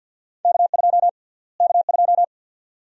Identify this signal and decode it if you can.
Morse code, heard as a keyed tone, about 46 wpm, 700 Hz. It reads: K2 K2
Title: USA Call Sign Prefixes - Repeated 2x - 50wpm